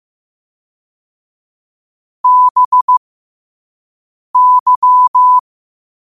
Morse code, heard as a keyed tone, about 15 wpm, spaced as for 5 wpm, 1000 Hz. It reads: BY
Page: New quotes every day in morse code at 5 Words per minute.